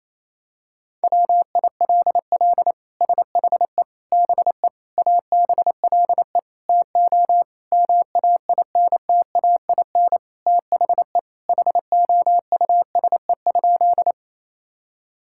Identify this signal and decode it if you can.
Morse code, heard as a keyed tone, about 28 wpm, 700 Hz. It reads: WILL SHE BE ABLE TO MAINTAIN THE HOUSE?
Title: Sentences from Top 1000 Words - Review Part 5 - 28wpm